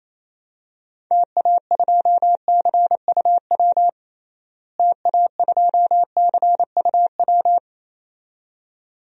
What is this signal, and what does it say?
Morse code, heard as a keyed tone, about 28 wpm, 700 Hz. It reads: TA2CUW TA2CUW
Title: DXCC-100 Call Signs - 6 Characters - Repeated 2x - 28wpm